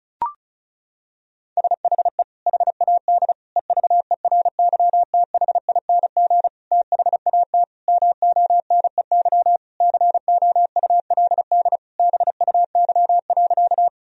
Morse code, heard as a keyed tone, about 36 wpm, 700 Hz. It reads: SHE HAD EVERYTHING THAT MONEY COULD BUY.
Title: Sentences from Top 1000 Words - Review Part 4 - 35wpm